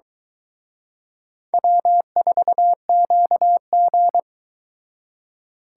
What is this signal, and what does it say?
Morse code, heard as a keyed tone, about 24 wpm, 700 Hz. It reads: W4QG